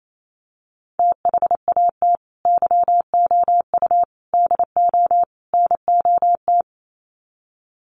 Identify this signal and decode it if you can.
Morse code, heard as a keyed tone, about 28 wpm, 700 Hz. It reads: THAT YOU DO NOT
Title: Sets of 4 Words 28wpm